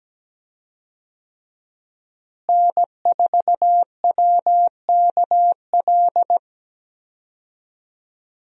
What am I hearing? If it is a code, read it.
Morse code, heard as a keyed tone, about 17 wpm, 700 Hz. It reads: N4WKL
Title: USA Call Signs - 5 Characters - Repeated 3x - 17wpm